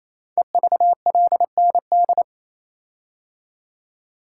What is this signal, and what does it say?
Morse code, heard as a keyed tone, about 28 wpm, 700 Hz. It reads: EVLND
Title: Code Groups - 5 Letter:Numbers - 28wpm